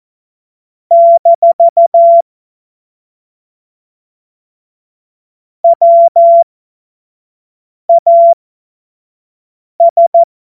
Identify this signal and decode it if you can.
Morse code, heard as a keyed tone, about 14 wpm, spaced as for 5 wpm, 680 Hz. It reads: - WAS
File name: Morsekurs / Witze / Witz 30 Januar 2024 25 F70